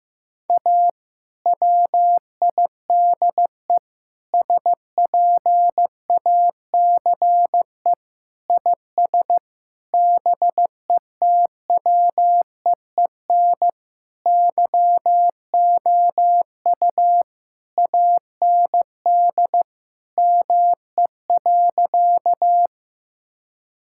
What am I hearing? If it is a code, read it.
Morse code, heard as a keyed tone, about 15 wpm, 700 Hz. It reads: A WIDE SPACE IS BETWEEN YOU AND ME.
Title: Sentences from Top 1000 Words - Review Part 6 - 15wpm